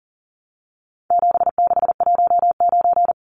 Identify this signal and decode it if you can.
Morse code, heard as a keyed tone, about 40 wpm, 700 Hz. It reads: 7619
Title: Numbers - 4-Digit 40wpm